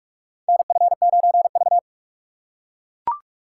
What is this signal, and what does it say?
Morse code, heard as a keyed tone, about 44 wpm, 700 Hz. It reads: NF9V
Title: USA Call Signs - 4 Characters - 4x Character Spacing - 45wpm